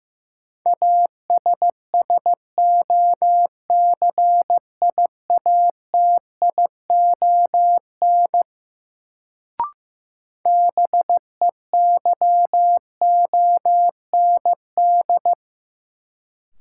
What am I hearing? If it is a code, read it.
Morse code, heard as a keyed tone, about 16 wpm, 700 Hz. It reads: ASSOCIATION BEYOND
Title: Sentences from Top 1100 Words - New Word Drill Alt - 15wpm